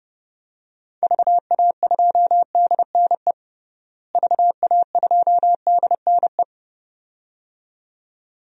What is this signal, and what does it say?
Morse code, heard as a keyed tone, about 32 wpm, 700 Hz. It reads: VA2DNE VA2DNE
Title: DXCC-100 Call Signs - 6 Characters - Repeated 2x - 30wpm